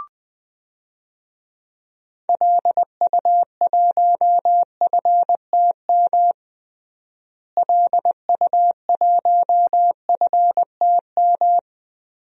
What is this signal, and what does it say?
Morse code, heard as a keyed tone, about 20 wpm, 700 Hz. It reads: LU1FTM LU1FTM
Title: DXCC-100 Call Signs - 6 Characters - Repeated 3x - 20wpm